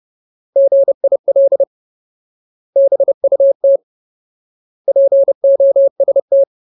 Morse code, heard as a keyed tone, about 30 wpm, 550 Hz. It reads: GIL BUT POST
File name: Ultimate Word List / 30WPM Ultimate